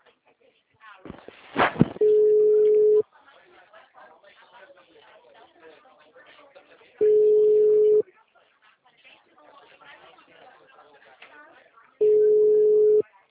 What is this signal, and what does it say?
Ringback tone (Europe, 425 Hz), with speech.